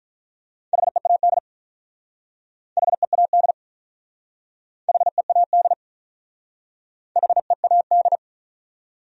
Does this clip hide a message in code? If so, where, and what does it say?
Yes: Morse code from 0.7 to 8.2 s, heard as a keyed tone, about 44 wpm, 700 Hz. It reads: HEAD HEAD HEAD HEAD